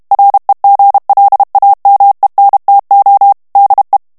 Morse code, heard as a keyed tone, about 32 wpm, 800 Hz. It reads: REGLAMENTO DE
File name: ESPAGNOL TEXTE EN CLAIR 2 VITESSE 30 MOTS MINUTES